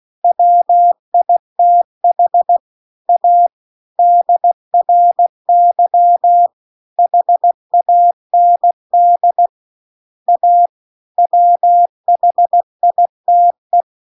Morse code, heard as a keyed tone, about 16 wpm, 700 Hz. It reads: WITH A DRY HAND A WHITE